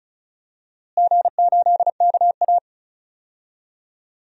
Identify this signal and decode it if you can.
Morse code, heard as a keyed tone, about 36 wpm, 700 Hz. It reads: G8KA
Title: DXCC-100 Call Signs - 4 Characters - Repeated 2x - 35wpm